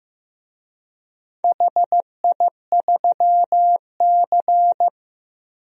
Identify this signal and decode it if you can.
Morse code, heard as a keyed tone, about 15 wpm, 700 Hz. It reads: HI3C
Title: DXCC-100 Call Signs - 4 Characters - Repeated 2x - 15wpm